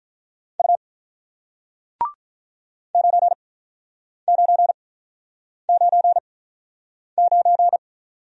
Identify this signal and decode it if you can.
Morse code, heard as a keyed tone, about 48 wpm, spaced as for 9 wpm, 700 Hz. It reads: U 9999